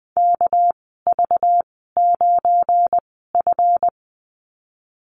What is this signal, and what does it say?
Morse code, heard as a keyed tone, about 20 wpm, spaced as for 14 wpm, 700 Hz. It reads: KV9F
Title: 238 Lesson 38g - Letter q - Call Signs 20wpm